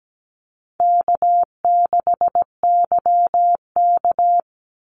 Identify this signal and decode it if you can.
Morse code, heard as a keyed tone, about 17 wpm, 700 Hz. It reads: K6YK
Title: NA SOTA Chaser Callsigns 17wpm